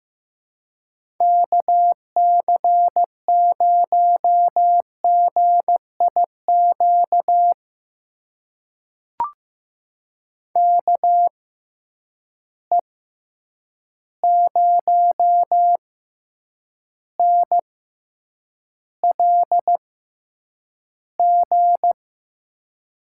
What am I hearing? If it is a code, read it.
Morse code, heard as a keyed tone, about 15 wpm, spaced as for 8 wpm, 700 Hz. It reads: KC0GIQ KE0NLG
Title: USA Call Signs - 6 Characters - 6x Character Spacing - 15wpm